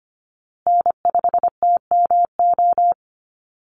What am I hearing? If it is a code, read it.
Morse code, heard as a keyed tone, about 25 wpm, 700 Hz. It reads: N5TMO
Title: USA Call Signs Repeated 3x 25wpm